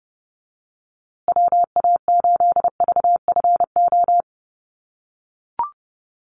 Morse code, heard as a keyed tone, about 30 wpm, 700 Hz. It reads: WA8VFO
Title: USA Call Signs Repeated 3x 30wpm